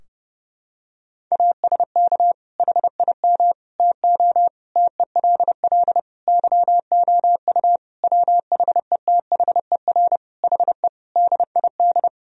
Morse code, heard as a keyed tone, about 32 wpm, 700 Hz. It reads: ASK HIM TO TELL YOU WHETHER HE DID
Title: Sentences from Top 1000 Words - Review Part 3 - 30wpm